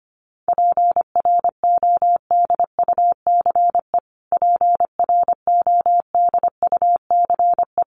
Morse code, heard as a keyed tone, about 25 wpm, 700 Hz. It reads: PRODUCE PRODUCE
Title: Top 500 Words Repeated 25wpm